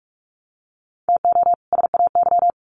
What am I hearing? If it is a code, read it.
Morse code, heard as a keyed tone, about 45 wpm, 700 Hz. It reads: TO SAY
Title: Sets of 2 Words 45wpm